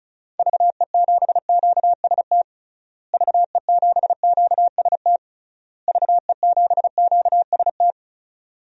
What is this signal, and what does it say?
Morse code, heard as a keyed tone, about 36 wpm, 700 Hz. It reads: VE7QST VE7QST VE7QST
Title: DXCC-100 Call Signs - 6 Characters - Repeated 3x - 35wpm